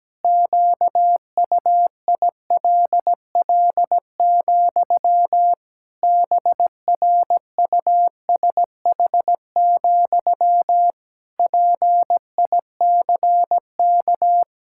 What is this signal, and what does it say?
Morse code, heard as a keyed tone, about 17 wpm, 700 Hz. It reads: QUILL, BRUSH, PICK
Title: Pangram Challenge v2 - 17wpm